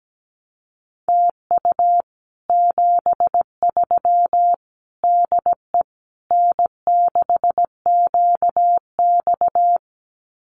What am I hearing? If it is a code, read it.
Morse code, heard as a keyed tone, about 17 wpm, 700 Hz. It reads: TU 73 DE N6QX
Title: Common QSO Phrases Encore 17wpm